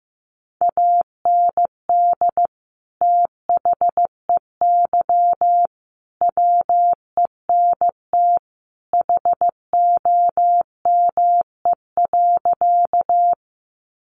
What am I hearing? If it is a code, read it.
Morse code, heard as a keyed tone, about 15 wpm, 700 Hz. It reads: AND THEY WENT HOME.
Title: Sentences from Top 200 Words 15wpm